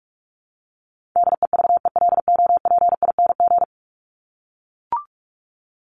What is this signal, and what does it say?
Morse code, heard as a keyed tone, about 44 wpm, 700 Hz. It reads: DEVELOPING
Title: ING Suffix - Top 100 Words with Base Words 45wpm